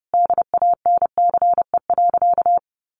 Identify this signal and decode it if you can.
Morse code, heard as a keyed tone, about 30 wpm, 700 Hz. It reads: DANCE.
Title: Sentences from Top 700 Words 30wpm